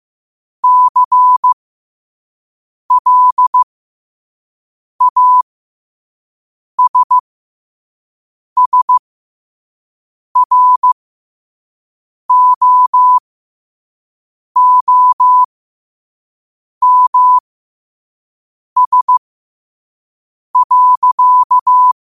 Morse code, heard as a keyed tone, about 15 wpm, spaced as for 5 wpm, 1000 Hz. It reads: CLASSROOMS.
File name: qotdfiles / 5wpm